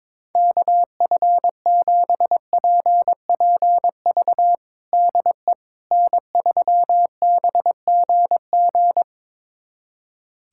Morse code, heard as a keyed tone, about 21 wpm, 700 Hz. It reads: KF7PPV DE N3BGG
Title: USA Callsign DE Callsign 22wpm